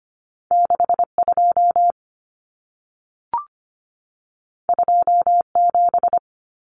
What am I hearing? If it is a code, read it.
Morse code, heard as a keyed tone, about 25 wpm, 700 Hz. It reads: 62 27
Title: Common Cut Numbers - 2-Digit 25wpm